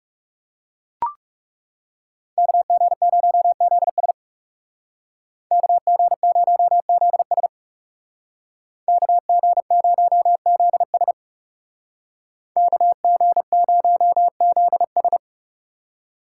Morse code, heard as a keyed tone, about 39 wpm, 700 Hz. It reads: KG0ZS KG0ZS KG0ZS KG0ZS